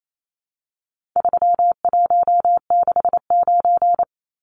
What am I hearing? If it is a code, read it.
Morse code, heard as a keyed tone, about 28 wpm, 700 Hz. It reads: 3169
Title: Numbers - 4-Digit 28wpm